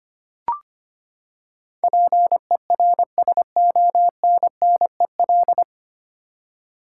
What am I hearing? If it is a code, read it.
Morse code, heard as a keyed tone, about 25 wpm, 700 Hz. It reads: PERSONNEL
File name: Nine Letter Words - 25wpm